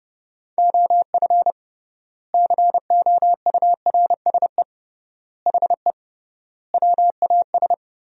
Morse code, heard as a keyed tone, about 32 wpm, 700 Hz. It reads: OF COURSE HE WAS
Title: 3x Word-Spacing - Sentences from Top 500 Words - 30wpm